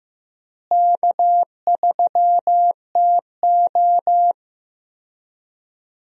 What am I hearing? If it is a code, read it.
Morse code, heard as a keyed tone, about 15 wpm, 700 Hz. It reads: K3TO
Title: USA Call Signs - 4 Characters - Repeated 3x - 15wpm